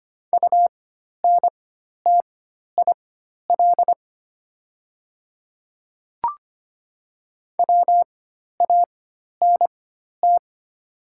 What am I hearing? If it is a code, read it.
Morse code, heard as a keyed tone, about 25 wpm, spaced as for 12 wpm, 700 Hz. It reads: UNTIL WANT
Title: 4x Character Spacing - Top 300 Words - 25wpm